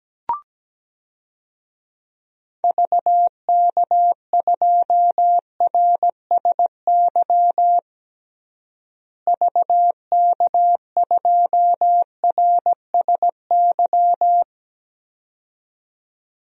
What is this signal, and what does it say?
Morse code, heard as a keyed tone, about 17 wpm, 700 Hz. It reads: VK2RSY VK2RSY
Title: DXCC-100 Call Signs - 6 Characters - Repeated 2x - 17wpm